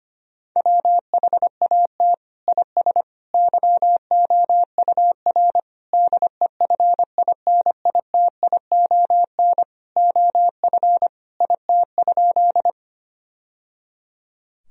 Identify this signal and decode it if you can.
Morse code, heard as a keyed tone, about 25 wpm, 700 Hz. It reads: WHAT IS YOUR DEFINITION OF IT?
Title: Sentences from Top 1000 Words - Review Part 4 - 25wpm